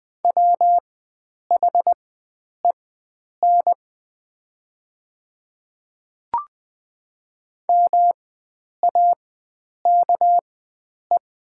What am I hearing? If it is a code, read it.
Morse code, heard as a keyed tone, about 20 wpm, spaced as for 9 wpm, 700 Hz. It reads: WHEN MAKE
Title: 4x Character Spacing - Top 500 Words - 20wpm